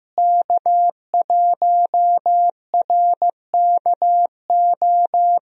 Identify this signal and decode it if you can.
Morse code, heard as a keyed tone, about 16 wpm, 700 Hz. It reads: K1RKO
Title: USA Call Signs - 5 Characters - Repeated 2x - 15wpm